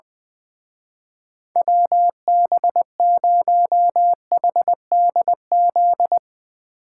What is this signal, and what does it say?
Morse code, heard as a keyed tone, about 22 wpm, 700 Hz. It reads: WB0HDZ
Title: Speed Racing - USA 2x3 Call Signs 20wpm